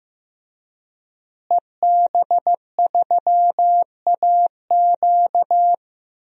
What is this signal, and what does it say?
Morse code, heard as a keyed tone, about 15 wpm, 700 Hz. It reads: EB3AQ
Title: DXCC-100 Call Signs - 5 Characters - Repeated 2x - 15wpm